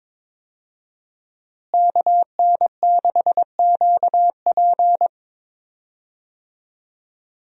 Morse code, heard as a keyed tone, about 22 wpm, 700 Hz. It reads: KN6QP
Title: USA Call Signs - 5 Characters - Repeated 3x - 22wpm